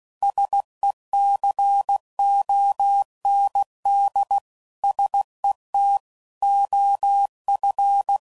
Morse code, heard as a keyed tone, about 16 wpm, 800 Hz. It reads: SECOND SET OF
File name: CW Code Groups @15WPM / Technical Subjects / PwrSupply